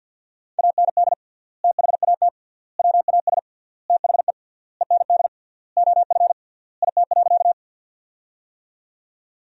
Morse code, heard as a keyed tone, about 50 wpm, 700 Hz. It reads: AND THAT WAS THE END OF IT.
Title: 3x Word-Spacing - Sentences from Top 200 Words - 50wpm